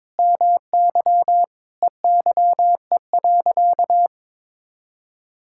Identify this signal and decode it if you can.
Morse code, heard as a keyed tone, about 22 wpm, 700 Hz. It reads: MY EYE.
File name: Sentences with 7-Word Phrases - 22wpm